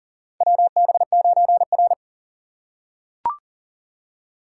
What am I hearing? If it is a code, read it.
Morse code, heard as a keyed tone, about 40 wpm, 700 Hz. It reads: WB9R
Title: USA Call Signs - 4 Characters - 4x Character Spacing - 40wpm